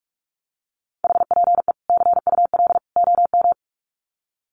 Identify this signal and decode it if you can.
Morse code, heard as a keyed tone, about 44 wpm, 700 Hz. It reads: HPE CUL OM
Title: Common QSO Phrases 45wpm